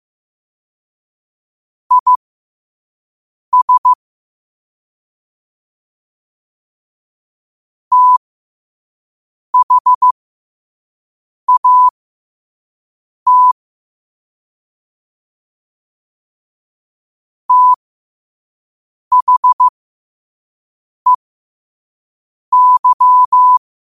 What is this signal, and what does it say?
Morse code, heard as a keyed tone, about 15 wpm, spaced as for 5 wpm, 1000 Hz. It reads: IS THAT THEY
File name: qotdfiles / 5wpm